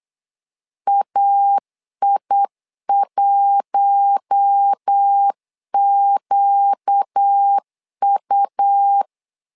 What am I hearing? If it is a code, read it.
Morse code, heard as a keyed tone, about 8 wpm, 800 Hz. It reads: AI1QU
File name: LEZIONI / 4 Q1 8wpm 3035